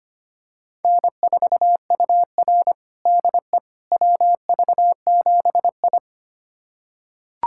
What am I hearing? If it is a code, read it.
Morse code, heard as a keyed tone, about 25 wpm, 700 Hz. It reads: N4UR DE WV7I
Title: USA Callsign DE Callsign - 4 Characters - 25wpm